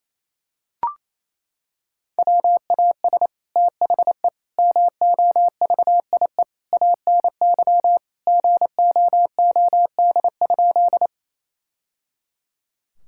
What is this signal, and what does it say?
Morse code, heard as a keyed tone, about 28 wpm, 700 Hz. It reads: WAS THE MOVIE ANY GOOD?
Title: Sentences from Top 1000 Words - Review Part 5 - 28wpm